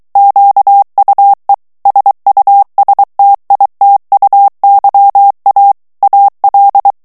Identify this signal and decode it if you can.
Morse code, heard as a keyed tone, about 23 wpm, 800 Hz. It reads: QUE SUSTITUYA AL
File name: morse mp3 / ESPAGNOL TEXTE EN CLAIR 2 VITESSE 22 MOTS MINUTES